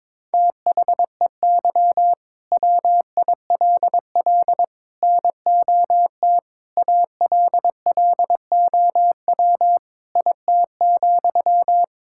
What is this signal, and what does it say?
Morse code, heard as a keyed tone, about 22 wpm, 700 Hz. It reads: THEY WILL NOT ALLOW IT,
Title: Sentences from Top 1000 Words - Review Part 5 - 22wpm